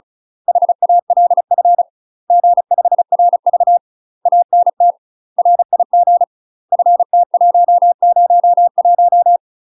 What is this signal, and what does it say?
Morse code, heard as a keyed tone, about 36 wpm, 700 Hz. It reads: HALF G5RV ANT RIG FT101
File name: UK MASTER QSOs TRAINING / 35WPM / UK MASTER QSOs 0031